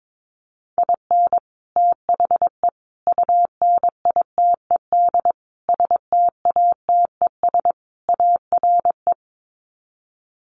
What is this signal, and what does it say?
Morse code, heard as a keyed tone, about 22 wpm, 700 Hz. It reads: IN THE UNITED STATES ARE
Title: Sets of 5 Words Encore 22wpm